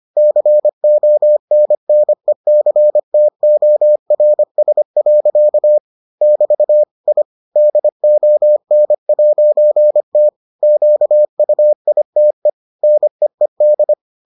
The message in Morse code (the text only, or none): CONNECTORS. <BT> I DON'T QUITE NEED